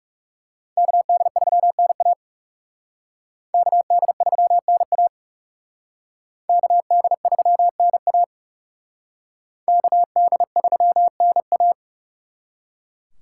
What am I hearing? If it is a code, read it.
Morse code, heard as a keyed tone, about 36 wpm, 700 Hz. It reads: KD3NA KD3NA KD3NA KD3NA
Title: Speed Racing - USA 2x2 Call Signs 30wpm